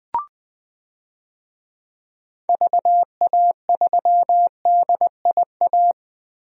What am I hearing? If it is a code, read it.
Morse code, heard as a keyed tone, about 21 wpm, 700 Hz. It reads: VA3DIA
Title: DXCC-100 Call Signs - 6 Characters - Repeated 2x - 20wpm